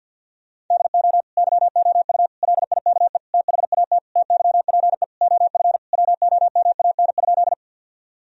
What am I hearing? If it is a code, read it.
Morse code, heard as a keyed tone, about 52 wpm, 700 Hz. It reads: DO YOU LIKE THAT TYPE OF WOMAN?
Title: Sentences from Top 1000 Words - Review Part 3 - 50wpm